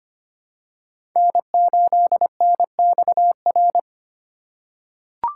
Morse code, heard as a keyed tone, about 25 wpm, 700 Hz. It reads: N8NXR
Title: USA Call Signs - 5 Characters - 4x Character Spacing - 25wpm